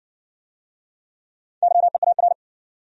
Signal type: Morse code

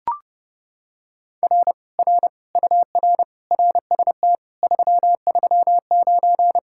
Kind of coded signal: Morse code

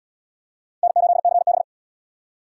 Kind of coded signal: Morse code

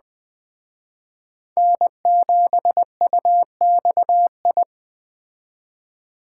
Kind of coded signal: Morse code

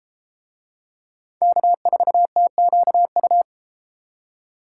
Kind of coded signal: Morse code

